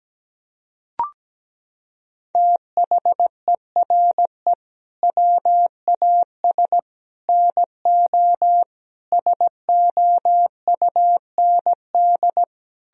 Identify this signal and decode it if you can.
Morse code, heard as a keyed tone, about 17 wpm, 700 Hz. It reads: THERE WAS NO SOUND
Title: Sentences from Top 1000 Words - Review Part 3 - 17wpm